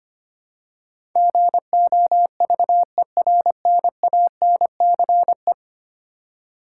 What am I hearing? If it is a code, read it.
Morse code, heard as a keyed tone, about 25 wpm, 700 Hz. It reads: GOVERNANCE
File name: Ten Letter Words - 25wpm